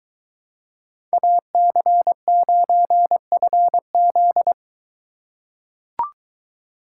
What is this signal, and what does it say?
Morse code, heard as a keyed tone, about 23 wpm, 700 Hz. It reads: AC9FZ